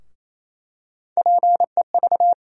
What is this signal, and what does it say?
Morse code, heard as a keyed tone, about 27 wpm, 700 Hz. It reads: PEV
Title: USA Call Sign Postfixes - 3 Letters 28wpm